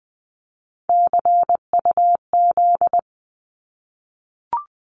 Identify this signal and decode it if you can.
Morse code, heard as a keyed tone, about 20 wpm, 700 Hz. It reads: CUZ
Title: Abbreviations 20wpm